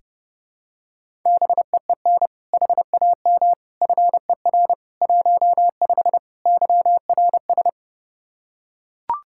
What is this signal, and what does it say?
Morse code, heard as a keyed tone, about 30 wpm, 700 Hz. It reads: BEEN HAM FER 15 YRS
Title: Practice QSO - Exchange 3 - More Info - 30wpm